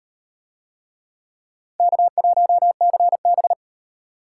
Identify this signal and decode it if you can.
Morse code, heard as a keyed tone, about 35 wpm, 700 Hz. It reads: K1CB